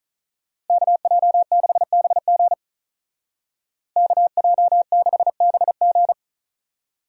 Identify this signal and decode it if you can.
Morse code, heard as a keyed tone, about 47 wpm, 700 Hz. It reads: KJ6BG KJ6BG